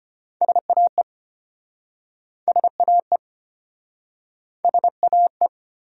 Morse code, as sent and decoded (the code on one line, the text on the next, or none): ... .- . / ... .- . / ... .- .
SAE SAE SAE